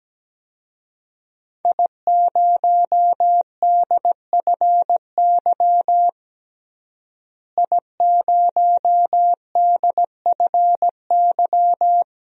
Morse code, heard as a keyed tone, about 17 wpm, 700 Hz. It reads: I0DFY I0DFY
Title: DXCC-100 Call Signs - 5 Characters - Repeated 3x - 17wpm